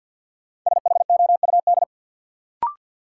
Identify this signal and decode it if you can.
Morse code, heard as a keyed tone, about 52 wpm, 700 Hz. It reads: SHOUD
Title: Sentences from Top 1000 Words - Review Part 4 - New Word Drill - 50wpm